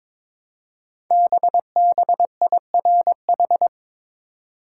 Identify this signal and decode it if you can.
Morse code, heard as a keyed tone, about 22 wpm, 700 Hz. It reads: BBIRH
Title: Code Groups - 5 Letters - 22wpm